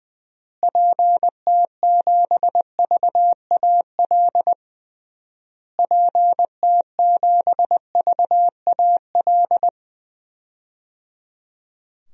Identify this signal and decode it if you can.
Morse code, heard as a keyed tone, about 20 wpm, 700 Hz. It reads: PT7VAL PT7VAL
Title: DXCC-100 Call Signs - 6 Characters - Repeated 3x - 20wpm